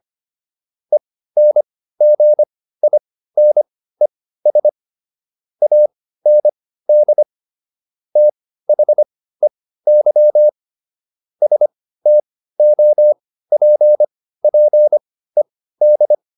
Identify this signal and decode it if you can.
Morse code, heard as a keyed tone, about 25 wpm, spaced as for 15 wpm, 600 Hz. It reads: ENGINES AND THEY STOPPED
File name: SS101 15